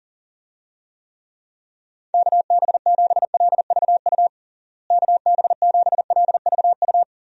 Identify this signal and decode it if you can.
Morse code, heard as a keyed tone, about 37 wpm, 700 Hz. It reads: KB7LVU KB7LVU
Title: USA Call Signs - 6 Characters - Repeated 3x - 40wpm